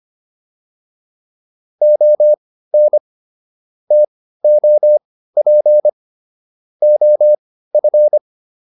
Morse code, heard as a keyed tone, about 25 wpm, spaced as for 15 wpm, 600 Hz. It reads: ON TOP OF